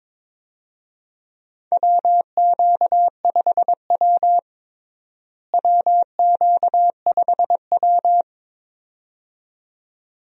Morse code, heard as a keyed tone, about 22 wpm, 700 Hz. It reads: WQ5W WQ5W
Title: USA Call Signs - 4 Characters - Repeated 2x - 22wpm